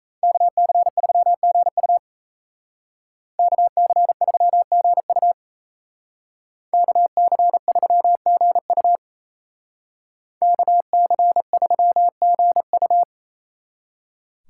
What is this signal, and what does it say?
Morse code, heard as a keyed tone, about 36 wpm, 700 Hz. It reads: KC3GU KC3GU KC3GU KC3GU